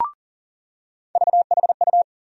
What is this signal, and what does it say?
Morse code, heard as a keyed tone, about 40 wpm, 700 Hz. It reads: VHU
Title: USA Call Sign Postfixes - 3 Letters 40wpm